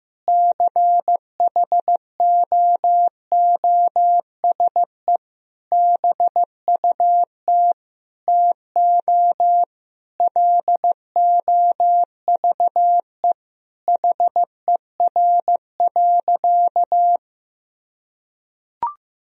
Morse code, heard as a keyed tone, about 15 wpm, 700 Hz. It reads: CHOOSE BUT TO LOVE HER.
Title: Sentences from Top 1000 Words - Review Part 4 - 15wpm